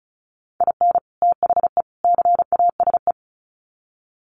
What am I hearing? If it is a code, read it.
Morse code, heard as a keyed tone, about 34 wpm, 700 Hz. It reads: IN THE CASE
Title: Sets of 3 Words 35wpm